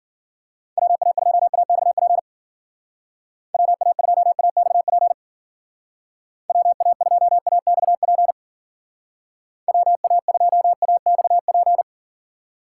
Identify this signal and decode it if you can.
Morse code, heard as a keyed tone, about 52 wpm, 700 Hz. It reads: WA2AXP WA2AXP WA2AXP WA2AXP